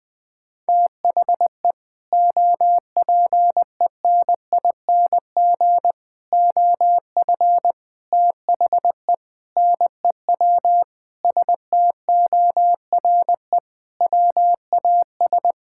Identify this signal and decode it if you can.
Morse code, heard as a keyed tone, about 20 wpm, 700 Hz. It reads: THE OPENING OF THE NEW STORE WAS
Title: Sentences with 3-Word Phrases - Spoken Beginning - 20wpm